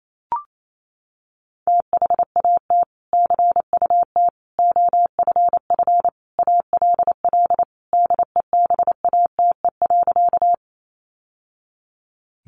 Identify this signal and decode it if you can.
Morse code, heard as a keyed tone, about 28 wpm, 700 Hz. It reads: THAT CUT OFF ALL DEBATE.
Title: Sentences from Top 1100 Words 28wpm